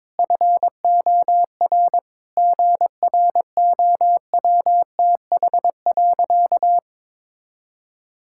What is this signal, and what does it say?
Morse code, heard as a keyed tone, about 22 wpm, 700 Hz. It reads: FOR GROWTH.
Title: Sentences from Top 1000 Words - Review Part 5 - 22wpm